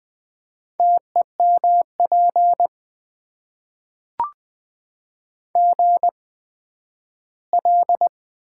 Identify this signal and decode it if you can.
Morse code, heard as a keyed tone, about 20 wpm, spaced as for 12 wpm, 700 Hz. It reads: TEMP GL
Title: Top 90 QSO Elements - 8x Character Spacing - 20wpm